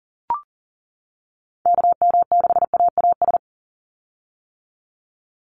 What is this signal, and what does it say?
Morse code, heard as a keyed tone, about 40 wpm, 700 Hz. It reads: KM6AAS
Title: USA Call Signs 40wpm